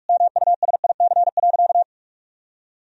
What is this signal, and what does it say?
Morse code, heard as a keyed tone, about 42 wpm, 700 Hz. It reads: MUSIC.
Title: Sentences from Top 1000 Words - Review Part 1 - 45wpm